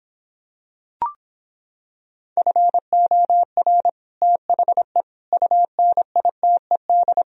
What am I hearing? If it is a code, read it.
Morse code, heard as a keyed tone, about 26 wpm, 700 Hz. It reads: FOR THE UNITED